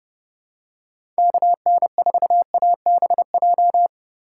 Morse code, heard as a keyed tone, about 30 wpm, 700 Hz. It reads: KN4ABJ
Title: USA Call Signs - 6 Characters - 3x Character Spacing - 30wpm